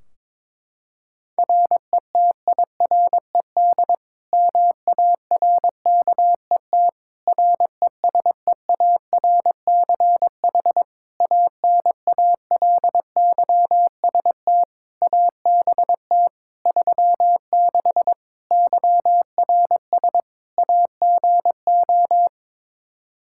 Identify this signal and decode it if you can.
Morse code, heard as a keyed tone, about 22 wpm, 700 Hz. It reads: RETIRED MARKET RESEARCH ANALYST ABT 36 YRS AGO
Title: Practice QSO - Exchange 3 - More Info - 22wpm